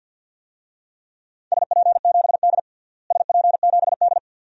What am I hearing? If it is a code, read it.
Morse code, heard as a keyed tone, about 52 wpm, 700 Hz. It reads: SP7D SP7D
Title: DXCC-100 Call Signs - 4 Characters - Repeated 2x - 50wpm